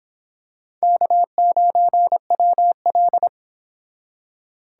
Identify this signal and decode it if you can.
Morse code, heard as a keyed tone, about 24 wpm, 700 Hz. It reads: K9WL